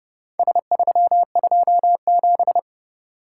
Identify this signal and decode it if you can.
Morse code, heard as a keyed tone, about 29 wpm, 700 Hz. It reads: S327